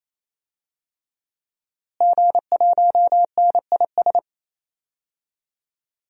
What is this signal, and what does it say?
Morse code, heard as a keyed tone, about 27 wpm, 700 Hz. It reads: G1NIS
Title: DXCC-100 Call Signs - 5 Characters - Repeated 3x - 28wpm